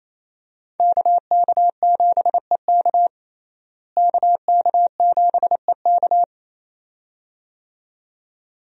Morse code, heard as a keyed tone, about 28 wpm, 700 Hz. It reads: KK7EK KK7EK
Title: USA Call Signs - 5 Characters - Repeated 2x - 28wpm